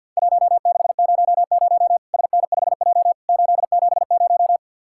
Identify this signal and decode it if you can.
Morse code, heard as a keyed tone, about 53 wpm, 700 Hz. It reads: 1600 SN5J 870
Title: Worked All Europe DX Contest - QTC Exchange - 50wpm